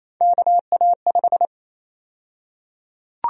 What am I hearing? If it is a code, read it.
Morse code, heard as a keyed tone, about 28 wpm, 700 Hz. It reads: KA5
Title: USA Call Sign Prefixes - Repeated 2x - 28wpm